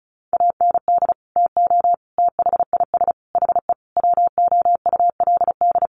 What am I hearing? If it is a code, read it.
Morse code, heard as a keyed tone, about 34 wpm, 700 Hz. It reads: AND TO THIS HE WOULD
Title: Sentences from Top 1200 Words 35wpm